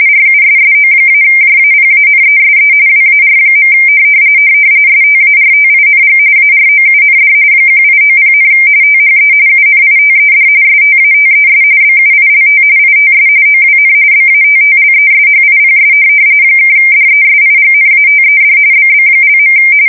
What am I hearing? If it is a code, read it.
RTTY, about 45.45 baud. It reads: RUSSO-JAPANESE WAR. STARTING AROUND 1890, THE UNITED STATES BUILT ITS OWN BATTLESHIPS (DESIGNATED BB).
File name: fsk170 / USS-New-Jersey